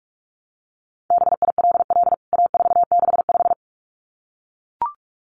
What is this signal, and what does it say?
Morse code, heard as a keyed tone, about 44 wpm, 700 Hz. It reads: BILL A465
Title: CWT Exchange 45wpm